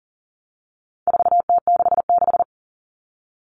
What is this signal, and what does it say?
Morse code, heard as a keyed tone, about 40 wpm, 700 Hz. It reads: 4T66
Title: All Cut Numbers - 4-Digit 40wpm